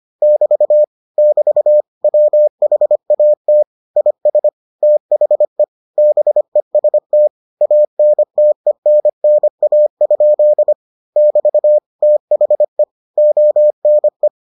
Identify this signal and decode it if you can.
Morse code, heard as a keyed tone, about 25 wpm, 600 Hz. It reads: <BT> <BT> WHAT IS THE BEST ANTENNA? <BT> THE ONE